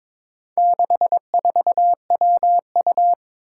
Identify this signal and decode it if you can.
Morse code, heard as a keyed tone, about 23 wpm, 700 Hz. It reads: 64WU
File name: practice / 22wpm / Four Letter-Number Combinations - 22wpm